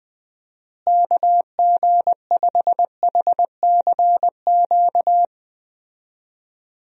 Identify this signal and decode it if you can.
Morse code, heard as a keyed tone, about 20 wpm, 700 Hz. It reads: KG5HCQ
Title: USA Call Signs - 6 Characters - Repeated 2x - 20wpm